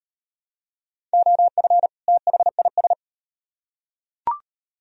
Morse code, heard as a keyed tone, about 38 wpm, 700 Hz. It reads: OF THIS